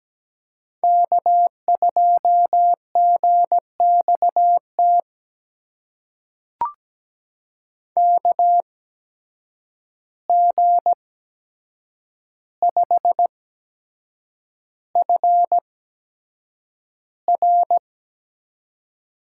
Morse code, heard as a keyed tone, about 17 wpm, spaced as for 7 wpm, 700 Hz. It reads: K2GXT KG5FR
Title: USA Call Signs - 8x Character Spacing - 17wpm